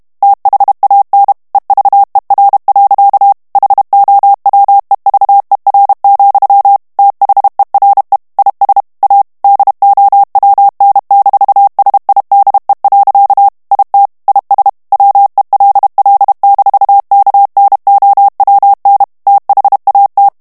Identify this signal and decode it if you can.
Morse code, heard as a keyed tone, about 32 wpm, 800 Hz. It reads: THAN EVER. HOWEVER, THERE IS A DOWN-SIDE. IT IS WELL-KNOWN THAT
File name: morse mp3 / ANGLAIS texte clair 1 vitesse 30 mots minute